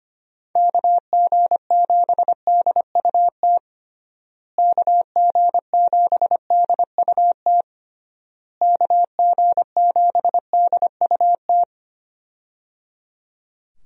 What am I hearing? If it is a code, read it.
Morse code, heard as a keyed tone, about 25 wpm, 700 Hz. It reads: KG7DUT KG7DUT KG7DUT
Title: USA Call Signs - 6 Characters - Repeated 3x - 25wpm